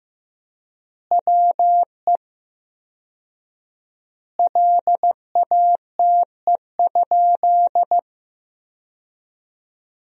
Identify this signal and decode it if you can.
Morse code, heard as a keyed tone, about 15 wpm, 700 Hz. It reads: WE LATE?
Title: 4x Word-Spacing - Sentences from Top 300 Words - 15wpm